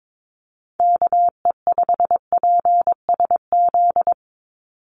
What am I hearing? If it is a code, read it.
Morse code, heard as a keyed tone, about 22 wpm, 700 Hz. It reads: KE5PSZ
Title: USA Call Signs 22wpm